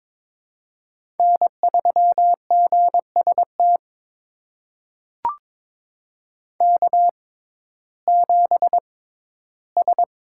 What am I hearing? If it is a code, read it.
Morse code, heard as a keyed tone, about 21 wpm, spaced as for 13 wpm, 700 Hz. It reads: N3GST K7S
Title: USA Call Signs - 5 Characters - 6x Character Spacing - 22wpm